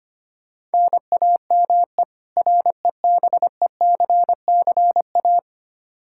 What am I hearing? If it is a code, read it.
Morse code, heard as a keyed tone, about 25 wpm, 700 Hz. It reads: NAME REBECCA
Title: K1USN SST QSO Practice 25wpm